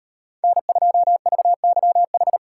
Morse code, heard as a keyed tone, about 38 wpm, 700 Hz. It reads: N2VYH